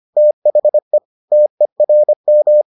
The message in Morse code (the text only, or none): THE TERM